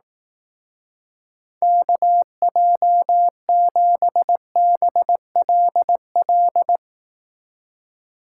Morse code, heard as a keyed tone, about 18 wpm, 700 Hz. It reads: KJ7BLL
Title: Speed Racing - USA 2x3 Call Signs 15wpm